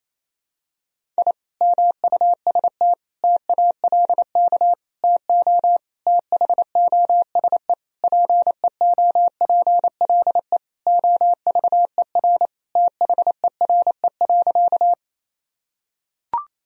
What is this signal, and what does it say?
Morse code, heard as a keyed tone, about 28 wpm, 700 Hz. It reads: I MUST TALK TO THOSE PEOPLE OVER THERE.
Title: Sentences from Top 1000 Words - Review Part 1 - 28wpm